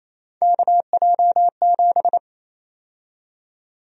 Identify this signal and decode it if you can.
Morse code, heard as a keyed tone, about 28 wpm, 700 Hz. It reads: KJ7
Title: USA Call Sign Prefixes - Repeated 2x - 28wpm